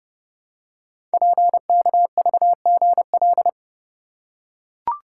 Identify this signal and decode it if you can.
Morse code, heard as a keyed tone, about 30 wpm, 700 Hz. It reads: PKVGL
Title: Code Groups - 5 Letters - 30wpm